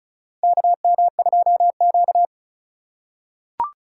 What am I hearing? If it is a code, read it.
Morse code, heard as a keyed tone, about 36 wpm, 700 Hz. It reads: KM2Q
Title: USA Call Signs - 6x Character Spacing - 35wpm